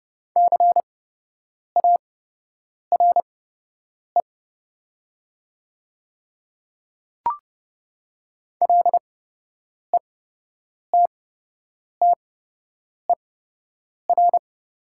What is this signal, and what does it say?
Morse code, heard as a keyed tone, about 30 wpm, spaced as for 8 wpm, 700 Hz. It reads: CARE LETTER
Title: 8x Character Spacing - Top 300 Words - 30wpm